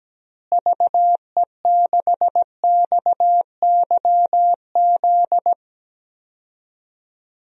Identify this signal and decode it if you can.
Morse code, heard as a keyed tone, about 17 wpm, 700 Hz. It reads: VE6XYZ
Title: DXCC-100 Call Signs - 6 Characters - Repeated 3x - 17wpm